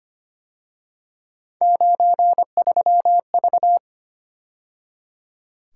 Morse code, heard as a keyed tone, about 31 wpm, 700 Hz. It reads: 93V
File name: Speed-Racing - Three Letter-Number Combinations - 25wpm